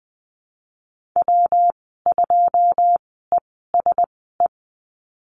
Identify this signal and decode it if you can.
Morse code, heard as a keyed tone, about 20 wpm, spaced as for 15 wpm, 700 Hz. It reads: W2ESE
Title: 186 Lesson 30g - Letter b - Call Signs 20wpm